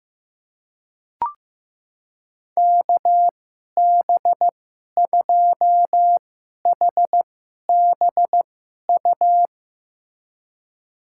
Morse code, heard as a keyed tone, about 15 wpm, spaced as for 11 wpm, 700 Hz. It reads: KB2HBU
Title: USA Call Signs - 6 Characters - 2x Character Spacing - 15wpm